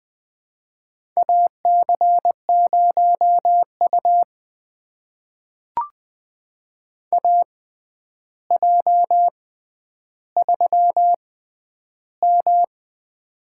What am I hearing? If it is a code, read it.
Morse code, heard as a keyed tone, about 21 wpm, spaced as for 10 wpm, 700 Hz. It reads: AC0U AJ3M
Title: USA Call Signs - 4 Characters - 6x Character Spacing - 20wpm